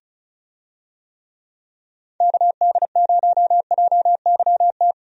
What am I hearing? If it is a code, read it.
Morse code, heard as a keyed tone, about 36 wpm, 700 Hz. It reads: KD0JYT
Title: USA Call Signs - 6 Characters - 35wpm